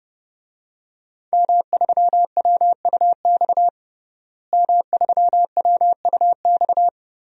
Morse code, heard as a keyed tone, about 30 wpm, 700 Hz. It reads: M3WUX M3WUX
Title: DXCC-100 Call Signs - 5 Characters - Repeated 3x - 30wpm